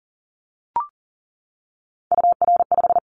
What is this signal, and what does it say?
Morse code, heard as a keyed tone, about 38 wpm, 700 Hz. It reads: UR5
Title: Callsign Prefixes for Top 25 DXCC - 40wpm